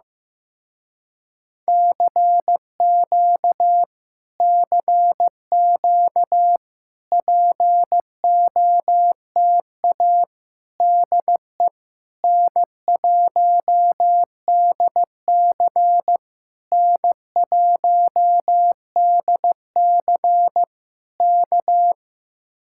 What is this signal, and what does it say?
Morse code, heard as a keyed tone, about 15 wpm, 700 Hz. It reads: CQ CQ POTA DE N1DC N1DC K
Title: Speed-Racing Common QSO Phrases Encore - 15wpm